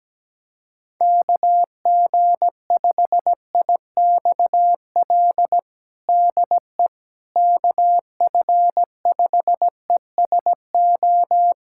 Morse code, heard as a keyed tone, about 17 wpm, 700 Hz. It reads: KG5IXL DE KF5ESO
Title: USA Callsign DE Callsign - 6 Characters - 17wpm